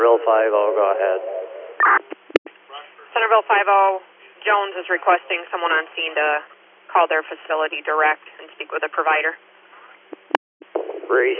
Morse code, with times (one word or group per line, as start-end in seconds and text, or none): none